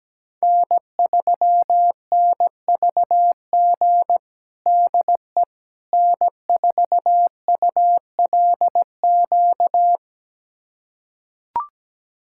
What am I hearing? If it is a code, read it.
Morse code, heard as a keyed tone, about 17 wpm, 700 Hz. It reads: N3NVG DE N4ULQ
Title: USA Callsign DE Callsign - 5 Characters - 17wpm